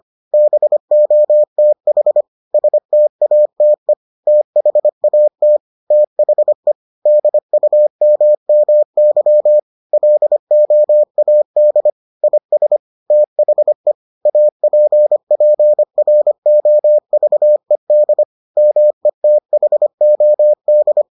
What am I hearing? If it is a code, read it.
Morse code, heard as a keyed tone, about 25 wpm, 600 Hz. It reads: BOTH STATE THAT THE DUMMY LOAD IS THE APPROVED METHOD